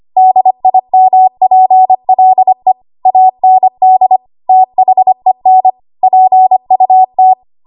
Morse code, heard as a keyed tone, about 25 wpm, 750 Hz. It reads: DIMPLE AND THEN PUT
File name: Archive / 25 WPM / 25WPM